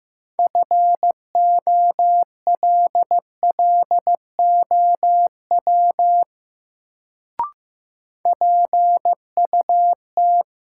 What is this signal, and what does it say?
Morse code, heard as a keyed tone, about 15 wpm, 700 Hz. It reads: FOLLOW PUT
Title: Sentences from Top 200 Words - New Word Drill - 15wpm